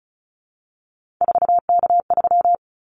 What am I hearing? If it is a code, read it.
Morse code, heard as a keyed tone, about 34 wpm, 700 Hz. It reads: 4K3
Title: Callsign Prefixes for Top 100 DXCC - 35wpm